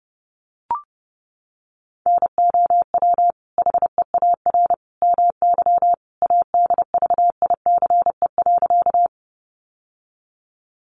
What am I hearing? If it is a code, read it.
Morse code, heard as a keyed tone, about 30 wpm, 700 Hz. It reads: NOW HEAR MY ADVICE.
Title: Sentences from Top 1100 Words 30wpm